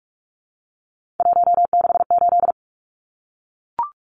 Morse code, heard as a keyed tone, about 44 wpm, 700 Hz. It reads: 168
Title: Numbers - 3-Digit 45wpm